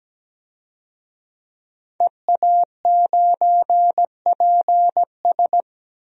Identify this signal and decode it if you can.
Morse code, heard as a keyed tone, about 17 wpm, 700 Hz. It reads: EA9PS
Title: DXCC-100 Call Signs - 5 Characters - 17wpm